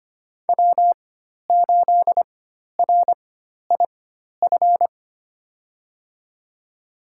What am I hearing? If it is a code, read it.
Morse code, heard as a keyed tone, about 25 wpm, spaced as for 12 wpm, 700 Hz. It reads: W8RIF
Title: USA Call Signs - 4x Character Spacing - 25wpm